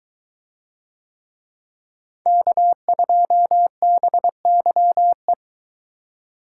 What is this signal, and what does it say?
Morse code, heard as a keyed tone, about 22 wpm, 700 Hz. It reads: K2BYE